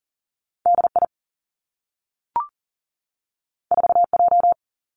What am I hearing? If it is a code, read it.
Morse code, heard as a keyed tone, about 40 wpm, 700 Hz. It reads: D I 4 J
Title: Two Challenging Letter-Number Combinations 40wpm